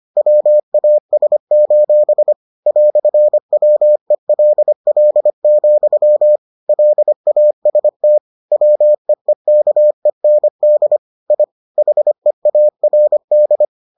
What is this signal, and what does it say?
Morse code, heard as a keyed tone, about 25 wpm, 600 Hz. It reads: WAS: "WELL, LAST WEEKEND I HEARD